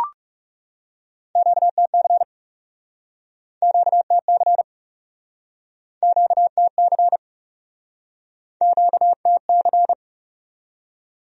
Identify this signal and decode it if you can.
Morse code, heard as a keyed tone, about 37 wpm, 700 Hz. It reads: QTC QTC QTC QTC
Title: Speed-Racing Q-Codes - 30wpm